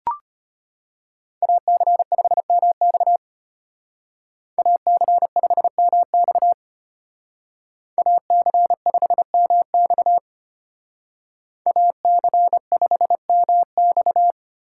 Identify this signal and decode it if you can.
Morse code, heard as a keyed tone, about 30 wpm, 700 Hz. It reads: AC5MX AC5MX AC5MX AC5MX
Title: Speed Racing - USA 2x2 Call Signs 25wpm